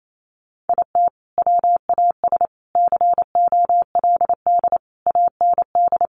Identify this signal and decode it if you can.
Morse code, heard as a keyed tone, about 28 wpm, 700 Hz. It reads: IT WAS COLD AND
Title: Sentences from Top 500 Words 28wpm